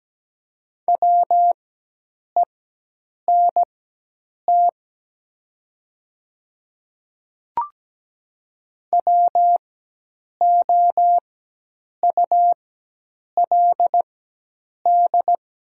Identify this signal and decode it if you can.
Morse code, heard as a keyed tone, about 17 wpm, spaced as for 8 wpm, 700 Hz. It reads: WENT WOULD
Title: 4x Character Spacing - Top 500 Words - 17wpm